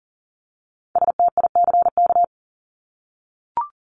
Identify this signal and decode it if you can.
Morse code, heard as a keyed tone, about 40 wpm, 700 Hz. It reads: STICK
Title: Top 1001-1250 Words 40wpm